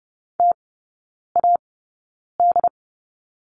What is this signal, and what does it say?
Morse code, heard as a keyed tone, about 30 wpm, spaced as for 9 wpm, 700 Hz. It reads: TAD
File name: 081 Lesson 14e - Number 2 - Three Characters 30wpm